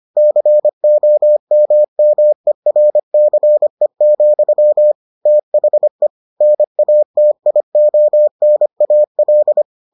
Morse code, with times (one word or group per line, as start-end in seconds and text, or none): 0.2-4.9 COMMERCE,
5.3-6.1 THE
6.4-9.6 NATIONAL